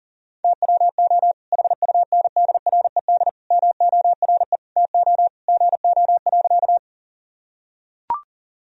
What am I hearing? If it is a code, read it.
Morse code, heard as a keyed tone, about 40 wpm, 700 Hz. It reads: TWO HUNDRED MORE TO GO.
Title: Sentences from Top 1000 Words - Review Part 2 - 40wpm